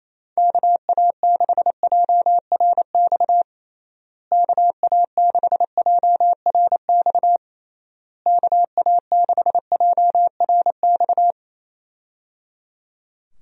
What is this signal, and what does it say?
Morse code, heard as a keyed tone, about 28 wpm, 700 Hz. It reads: KA6JRX KA6JRX KA6JRX
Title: USA Call Signs - 6 Characters - Repeated 3x - 28wpm